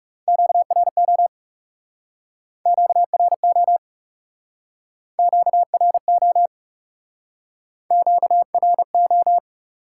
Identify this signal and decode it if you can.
Morse code, heard as a keyed tone, about 39 wpm, 700 Hz. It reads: QRO QRO QRO QRO